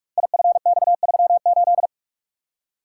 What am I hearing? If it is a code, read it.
Morse code, heard as a keyed tone, about 44 wpm, 700 Hz. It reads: IFX38
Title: Code Groups - 5 Letter:Numbers - 45wpm